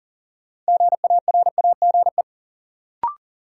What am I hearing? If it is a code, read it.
Morse code, heard as a keyed tone, about 40 wpm, 700 Hz. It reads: GARAGE
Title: Sentences from Top 1600 Words - New Word Drill Alt - 40wpm